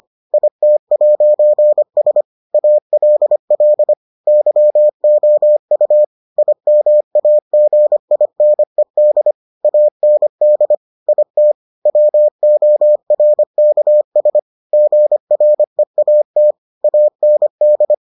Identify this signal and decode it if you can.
Morse code, heard as a keyed tone, about 25 wpm, 600 Hz. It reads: IT'S ALL YOU IMAGINED AND IT WORKS GREAT AND